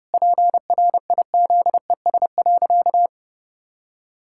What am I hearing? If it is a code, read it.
Morse code, heard as a keyed tone, about 32 wpm, 700 Hz. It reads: PRIZES.
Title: Pangram Challenge v2 - 30wpm